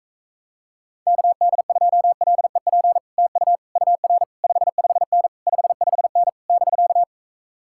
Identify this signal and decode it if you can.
Morse code, heard as a keyed tone, about 37 wpm, 700 Hz. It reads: KD2LEP TU UR 55N 55N <BK>